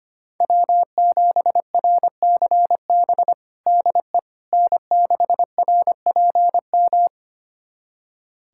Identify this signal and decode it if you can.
Morse code, heard as a keyed tone, about 25 wpm, 700 Hz. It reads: W7RCB DE N6RPM
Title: USA Callsign DE Callsign - 5 Characters - 25wpm